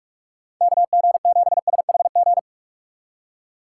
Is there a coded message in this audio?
Yes: Morse code from 0.6 to 2.4 s, heard as a keyed tone, about 59 wpm, 700 Hz. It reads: KG7SHG